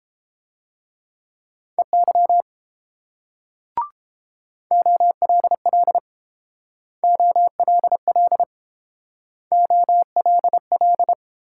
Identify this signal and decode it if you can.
Morse code, heard as a keyed tone, about 29 wpm, 700 Hz. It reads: EY OLL OLL OLL